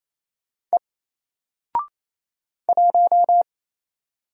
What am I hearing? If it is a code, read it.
Morse code, heard as a keyed tone, about 27 wpm, 700 Hz. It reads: E 1
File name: Single-Letter-Number-28wpm